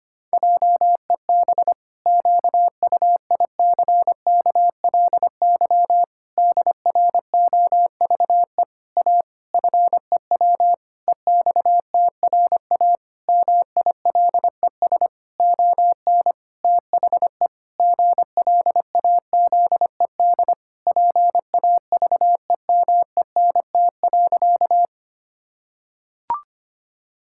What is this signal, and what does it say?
Morse code, heard as a keyed tone, about 25 wpm, 700 Hz. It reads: JEB QUICKLY DROVE A FEW EXTRA MILES ON THE GLAZED PAVEMENT.
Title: Pangram Challenge v2 - 25wpm